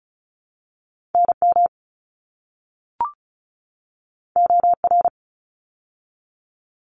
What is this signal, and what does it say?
Morse code, heard as a keyed tone, about 34 wpm, 700 Hz. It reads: NM OR
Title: ARRL Sections 35wpm